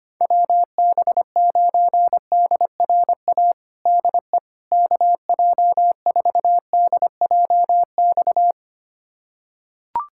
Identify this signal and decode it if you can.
Morse code, heard as a keyed tone, about 25 wpm, 700 Hz. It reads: WB9DRA DE KJ4DJX
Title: USA Callsign DE Callsign - 6 Characters - 25wpm